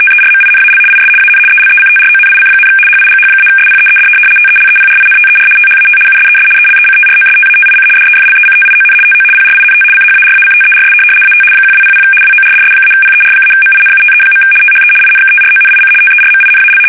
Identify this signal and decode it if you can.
RTTY, about 75 baud. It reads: TERMINAL IS AN ELECTRONIC, HIGH-SPEED TELEPRINTER THAT USES ELECTROSTATIC DEFLECTION TO PRINT CHARACTERS AT 1200 WORDS-PER-MINUTE. THE INKTRONIC TERMINAL
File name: fsk850-75B / teletype-corp